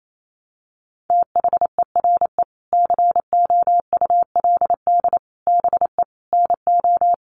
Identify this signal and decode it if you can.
Morse code, heard as a keyed tone, about 28 wpm, 700 Hz. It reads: THERE COULD BE NO
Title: Sentences from Top 700 Words 28wpm